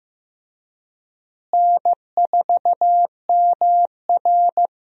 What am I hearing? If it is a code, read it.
Morse code, heard as a keyed tone, about 16 wpm, 700 Hz. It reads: N4MR
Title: USA Call Signs - 4 Characters - Repeated 3x - 15wpm